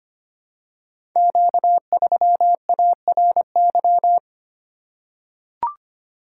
Morse code, heard as a keyed tone, about 25 wpm, 700 Hz. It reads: Q3ARY
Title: Code Groups - 5 Letter:Numbers - 25wpm